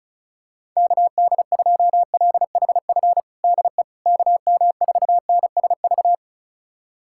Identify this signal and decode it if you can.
Morse code, heard as a keyed tone, about 36 wpm, 700 Hz. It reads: KD2LHF DE KM4NSV
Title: USA Callsign DE Callsign - 6 Characters - 35wpm